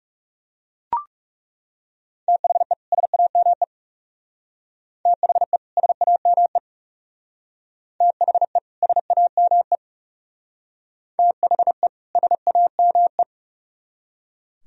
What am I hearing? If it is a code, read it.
Morse code, heard as a keyed tone, about 39 wpm, 700 Hz. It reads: THE SAME THE SAME THE SAME THE SAME